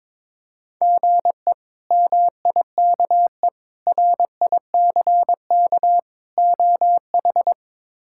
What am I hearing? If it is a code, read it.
Morse code, heard as a keyed tone, about 22 wpm, 700 Hz. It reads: GE MIKE RICK OH
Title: K1USN SST QSO Practice - 22wpm